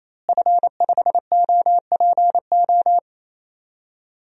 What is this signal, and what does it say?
Morse code, heard as a keyed tone, about 28 wpm, 700 Hz. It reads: F5OPO
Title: DXCC-100 Call Signs - 5 Characters - Repeated 3x - 28wpm